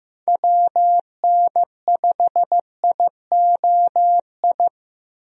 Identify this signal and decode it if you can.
Morse code, heard as a keyed tone, about 15 wpm, 700 Hz. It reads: WN5IOI
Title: USA Call Signs - 6 Characters - Repeated 2x - 15wpm